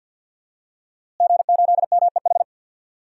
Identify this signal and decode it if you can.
Morse code, heard as a keyed tone, about 46 wpm, 700 Hz. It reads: G8MEH
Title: DXCC-100 Call Signs - 5 Characters - Repeated 3x - 50wpm